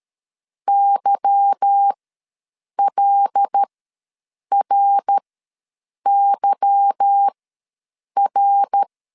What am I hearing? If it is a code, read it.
Morse code, heard as a keyed tone, about 13 wpm, spaced as for 7 wpm, 800 Hz. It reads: YLRYR
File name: LEZIONI / 8 YJ45 / 8 Y ws 12wpm 3409